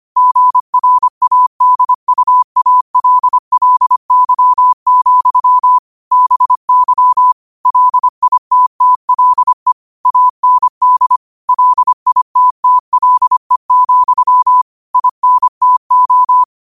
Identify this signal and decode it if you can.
Morse code, heard as a keyed tone, about 25 wpm, 1000 Hz. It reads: GRADUALLY, BY LITTLE AND LITTLE, INTO